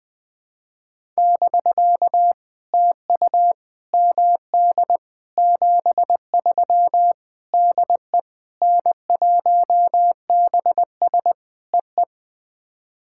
Morse code, heard as a keyed tone, about 20 wpm, 700 Hz. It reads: <BK> TU MD 73 DE N1BS EE
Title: POTA QSO Practice - 20wpm